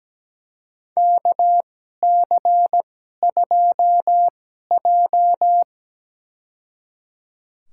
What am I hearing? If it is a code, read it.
Morse code, heard as a keyed tone, about 17 wpm, spaced as for 12 wpm, 700 Hz. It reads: KC2J
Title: USA Call Signs - 2x Character Spacing - 17wpm